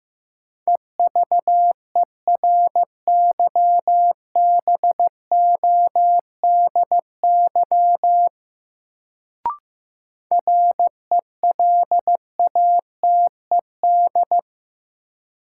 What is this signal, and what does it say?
Morse code, heard as a keyed tone, about 15 wpm, 700 Hz. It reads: EVERYBODY RELATED
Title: Sentences from Top 1400 Words - New Word Drill Alt - 15wpm